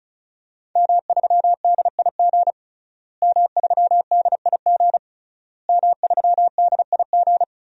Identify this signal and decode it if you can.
Morse code, heard as a keyed tone, about 36 wpm, 700 Hz. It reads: M3DIG M3DIG M3DIG
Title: DXCC-100 Call Signs - 5 Characters - Repeated 3x - 35wpm